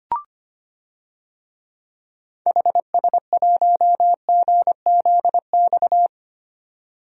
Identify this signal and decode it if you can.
Morse code, heard as a keyed tone, about 25 wpm, 700 Hz. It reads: HS1GZX
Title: DXCC-100 Call Signs - 6 Characters - 25wpm